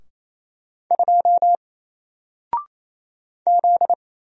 Morse code, heard as a keyed tone, about 27 wpm, 700 Hz. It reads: 2 Z
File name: Single-Letter-Number-28wpm